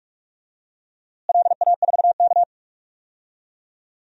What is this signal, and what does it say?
Morse code, heard as a keyed tone, about 47 wpm, 700 Hz. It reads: LA4K
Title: DXCC-100 Call Signs - 4 Characters - Repeated 2x - 45wpm